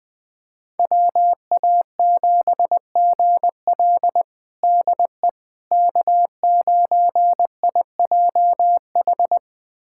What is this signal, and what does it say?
Morse code, heard as a keyed tone, about 21 wpm, 700 Hz. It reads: WA7GL DE K9IJH
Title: USA Callsign DE Callsign - 5 Characters - 20wpm